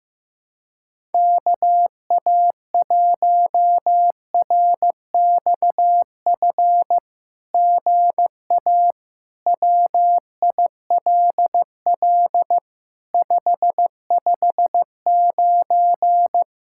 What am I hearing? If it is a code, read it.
Morse code, heard as a keyed tone, about 15 wpm, 700 Hz. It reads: KA1RXF GA WILL 559
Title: SOTA QSO Practice 15wpm